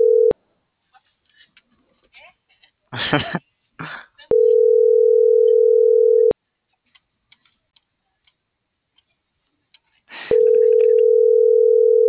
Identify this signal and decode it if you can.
Ringback tone (North America), 440 + 480 Hz.